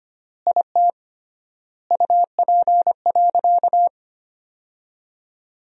Morse code, heard as a keyed tone, about 25 wpm, 700 Hz. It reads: IT UP.
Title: 3x Word-Spacing - Sentences from Top 100 Words - 25wpm